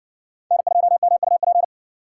Morse code, heard as a keyed tone, about 60 wpm, 700 Hz. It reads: N2MUP